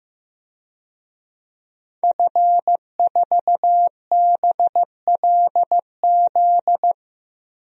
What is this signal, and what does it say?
Morse code, heard as a keyed tone, about 15 wpm, 700 Hz. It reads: F4BLZ
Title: DXCC-100 Call Signs - 5 Characters - Repeated 3x - 15wpm